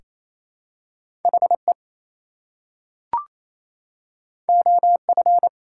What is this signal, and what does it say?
Morse code, heard as a keyed tone, about 27 wpm, 700 Hz. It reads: HE OF
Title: Two Letter Words - 28wpm